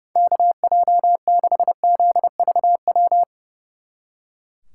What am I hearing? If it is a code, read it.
Morse code, heard as a keyed tone, about 37 wpm, 700 Hz. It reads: KJ6ZVW
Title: Speed Racing - USA Call Signs 30wpm